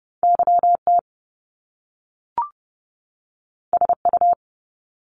Morse code, heard as a keyed tone, about 30 wpm, 700 Hz. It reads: YT SU
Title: Top Two Letter Combinations in Words 30wpm